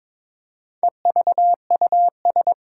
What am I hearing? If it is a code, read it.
Morse code, heard as a keyed tone, about 22 wpm, 700 Hz. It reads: EVUS
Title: Four Challenging Letter-Number Combinations 22wpm